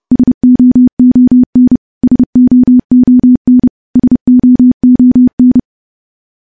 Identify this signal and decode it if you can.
Morse code, heard as a keyed tone, about 28 wpm, 250 Hz. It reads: SOON SOON SOON
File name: morse-training-30WPM